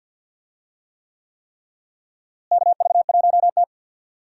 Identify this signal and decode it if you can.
Morse code, heard as a keyed tone, about 52 wpm, 700 Hz. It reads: KV1T
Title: USA Call Signs - 4 Characters - Repeated 3x - 50wpm